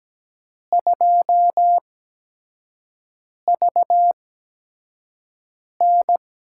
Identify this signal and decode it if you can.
Morse code, heard as a keyed tone, about 17 wpm, spaced as for 5 wpm, 700 Hz. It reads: 2VN
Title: USA Call Signs - 8x Character Spacing - 17wpm